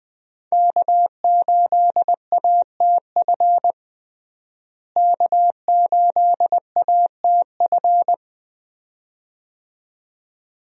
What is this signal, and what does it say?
Morse code, heard as a keyed tone, about 20 wpm, 700 Hz. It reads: K8ATF K8ATF
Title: USA Call Signs - 5 Characters - Repeated 2x - 20wpm